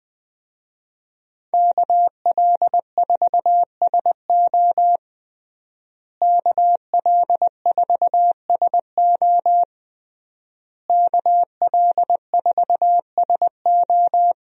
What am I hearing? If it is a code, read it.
Morse code, heard as a keyed tone, about 20 wpm, 700 Hz. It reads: KL4SO KL4SO KL4SO
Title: USA Call Signs - 5 Characters - Repeated 3x - 20wpm